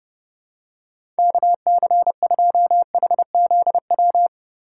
Morse code, heard as a keyed tone, about 30 wpm, 700 Hz. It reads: KC2HZW
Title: USA Call Signs - 6 Characters - 3x Character Spacing - 30wpm